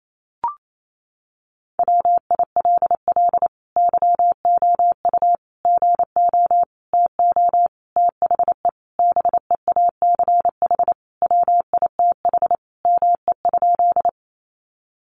Morse code, heard as a keyed tone, about 28 wpm, 700 Hz. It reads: WILL YOU GO TO THE BEACH WITH ME?
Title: Sentences from Top 1500 Words 28wpm